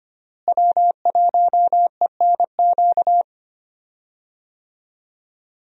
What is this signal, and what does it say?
Morse code, heard as a keyed tone, about 25 wpm, 700 Hz. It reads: W1ENQ
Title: USA Call Signs - 5 Characters - Repeated 3x - 25wpm